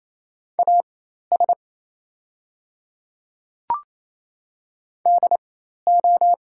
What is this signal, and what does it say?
Morse code, heard as a keyed tone, about 28 wpm, spaced as for 13 wpm, 700 Hz. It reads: AS DO
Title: 4x Character Spacing - Top 100 Words - 28wpm